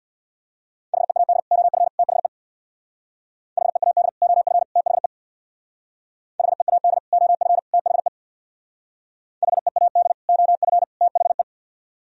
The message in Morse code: HEAD OF THE HEAD OF THE HEAD OF THE HEAD OF THE

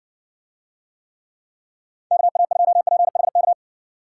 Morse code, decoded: KA2PHK

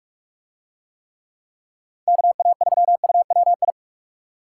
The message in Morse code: KA3UWI

